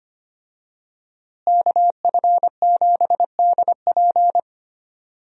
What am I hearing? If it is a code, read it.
Morse code, heard as a keyed tone, about 25 wpm, 700 Hz. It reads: KF7DP
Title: USA Call Signs - 5 Characters - Repeated 2x - 25wpm